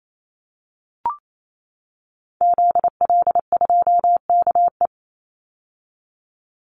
Morse code, heard as a keyed tone, about 28 wpm, 700 Hz. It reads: ZL2KE
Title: Call Signs - 28wpm